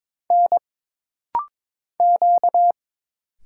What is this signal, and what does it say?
Morse code, heard as a keyed tone, about 22 wpm, 700 Hz. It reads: N Q
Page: This rapid-fire timing is geared to push your ICR (Instant Character Recognition) skills.